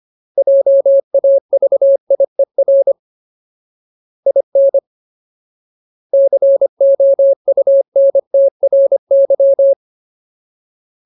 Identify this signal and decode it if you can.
Morse code, heard as a keyed tone, about 25 wpm, 550 Hz. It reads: JAVIER IN COUNTRY
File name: Ultimate Word List / 25WPM Ultimate